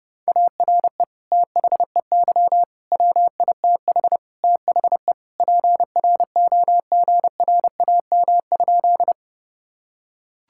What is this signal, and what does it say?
Morse code, heard as a keyed tone, about 32 wpm, 700 Hz. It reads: ARE THEY WITH THE PROGRAM?
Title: Sentences from Top 1000 Words - Review Part 5 - 30wpm